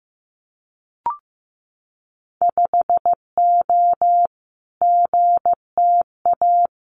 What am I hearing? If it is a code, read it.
Morse code, heard as a keyed tone, about 15 wpm, 700 Hz. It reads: 5O GTA
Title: Winter Field Day Exchanges Encore 15wpm